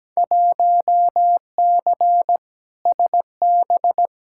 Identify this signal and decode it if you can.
Morse code, heard as a keyed tone, about 17 wpm, 700 Hz. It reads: 1C SB
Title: Field Day Exchanges Encore - 17wpm